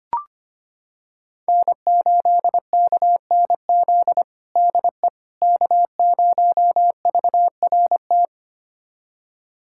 Morse code, heard as a keyed tone, about 25 wpm, 700 Hz. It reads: N8KNZ DE K0VRT
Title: USA Callsign DE Callsign - 5 Characters - 25wpm